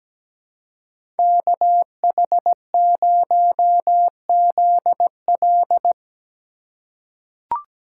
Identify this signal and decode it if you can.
Morse code, heard as a keyed tone, about 17 wpm, 700 Hz. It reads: KH0ZL
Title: USA Call Signs - 5 Characters - 3x Character Spacing - 17wpm